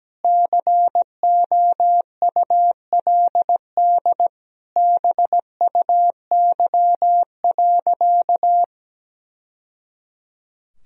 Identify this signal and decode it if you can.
Morse code, heard as a keyed tone, about 17 wpm, 700 Hz. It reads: COULD BUY.
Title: Sentences from Top 1000 Words - Review Part 4 - 17wpm